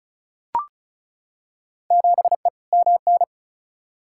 Morse code, heard as a keyed tone, about 34 wpm, 700 Hz. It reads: 7E MN
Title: Field Day Exchanges Encore - 35wpm